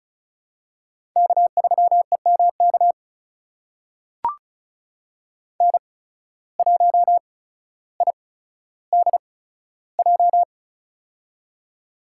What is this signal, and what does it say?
Morse code, heard as a keyed tone, about 34 wpm, spaced as for 15 wpm, 700 Hz. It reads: K3EMK N1IDJ
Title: USA Call Signs - 5 Characters - 8x Character Spacing - 35wpm